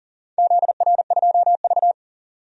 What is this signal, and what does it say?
Morse code, heard as a keyed tone, about 40 wpm, 700 Hz. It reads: ZR2V
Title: DXCC-100 Call Signs - 4 Characters - 40wpm